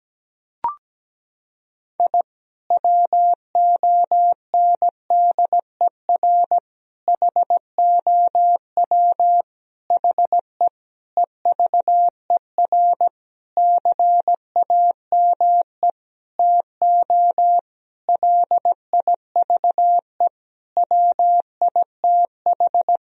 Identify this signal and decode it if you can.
Morse code, heard as a keyed tone, about 17 wpm, 700 Hz. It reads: I WONDER HOW HE EVER CAME TO LIVE WITH
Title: Sentences from Top 1000 Words - Review Part 3 - 17wpm